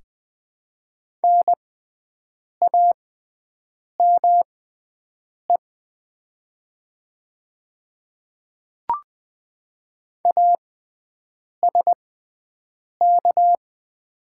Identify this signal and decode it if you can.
Morse code, heard as a keyed tone, about 20 wpm, spaced as for 7 wpm, 700 Hz. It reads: NAME ASK
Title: 6x Character Spacing - Top 300 Words - 20wpm